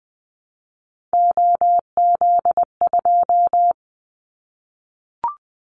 Morse code, heard as a keyed tone, about 20 wpm, 700 Hz. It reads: OZ2
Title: Callsign Prefixes for Top 50 DXCC - 20wpm